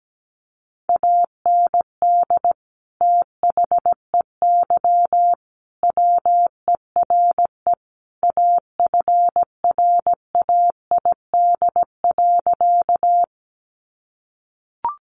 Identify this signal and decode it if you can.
Morse code, heard as a keyed tone, about 17 wpm, 700 Hz. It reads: AND THEY WERE AFRAID.
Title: Sentences from Top 1500 Words 17wpm